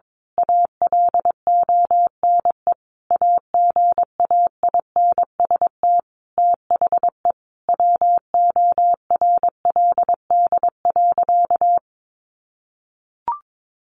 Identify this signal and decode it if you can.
Morse code, heard as a keyed tone, about 22 wpm, 700 Hz. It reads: ALONE AGAINST THE WORLD.
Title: Sentences from Top 1400 Words 22wpm